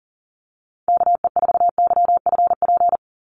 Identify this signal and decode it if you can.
Morse code, heard as a keyed tone, about 40 wpm, 700 Hz. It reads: KE4YFP
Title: USA Call Signs 40wpm